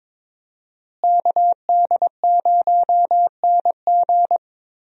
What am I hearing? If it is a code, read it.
Morse code, heard as a keyed tone, about 22 wpm, 700 Hz. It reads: KD0NG
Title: USA Call Signs - 5 Characters - Repeated 3x - 22wpm